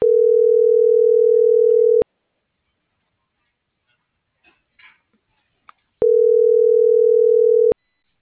Ringback tone (North America), 440 + 480 Hz.
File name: CENTRAL MULTIMODAL / out josimar.cruz - .77848